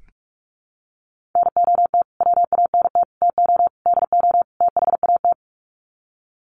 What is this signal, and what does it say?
Morse code, heard as a keyed tone, about 44 wpm, 700 Hz. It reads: NOT WANT TO DO THAT
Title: Sets of 5 Words Encore 45wpm